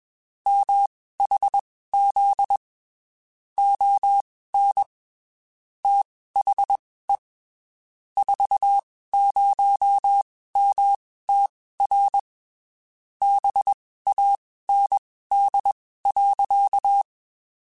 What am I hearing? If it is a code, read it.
Morse code, heard as a keyed tone, about 21 wpm, spaced as for 15 wpm, 800 Hz. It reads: MHZ ON THE 40MTR BAND.
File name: (spaced) / QSOs Spaced-mp3s / kc7sac @20 sp